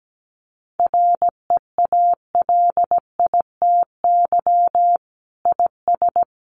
Morse code, heard as a keyed tone, about 17 wpm, 700 Hz. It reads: REALITY IS
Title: Sentences from Top 1200 Words 17wpm